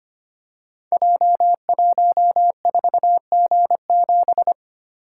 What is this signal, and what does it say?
Morse code, heard as a keyed tone, about 25 wpm, 700 Hz. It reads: J14G7
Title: Code Groups - 5 Letter:Numbers - 25wpm